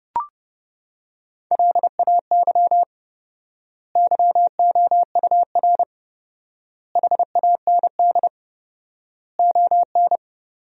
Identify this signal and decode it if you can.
Morse code, heard as a keyed tone, about 30 wpm, 700 Hz. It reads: LAY YOUR HAND ON
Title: 4x Word-Spacing - Sentences from Top 400 Words - 30wpm